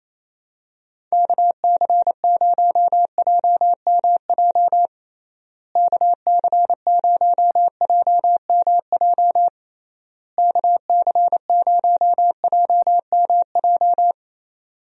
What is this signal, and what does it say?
Morse code, heard as a keyed tone, about 28 wpm, 700 Hz. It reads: KC0JMJ KC0JMJ KC0JMJ
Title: USA Call Signs - 6 Characters - Repeated 3x - 28wpm